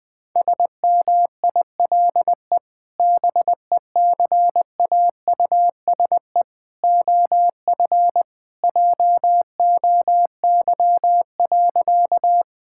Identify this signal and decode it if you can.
Morse code, heard as a keyed tone, about 21 wpm, 700 Hz. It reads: SMILE BECAUSE OF JOY.
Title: Sentences with 2-Word Phrases - 20wpm